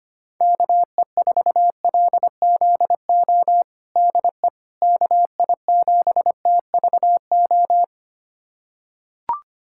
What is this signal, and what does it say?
Morse code, heard as a keyed tone, about 25 wpm, 700 Hz. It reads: KE4LZO DE KI7TVO
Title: USA Callsign DE Callsign - 6 Characters - 25wpm